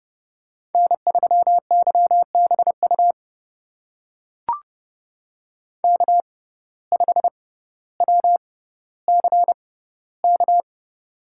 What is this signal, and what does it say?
Morse code, heard as a keyed tone, about 31 wpm, spaced as for 15 wpm, 700 Hz. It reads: N3YBU K5WCK
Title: USA Call Signs - 5 Characters - 6x Character Spacing - 30wpm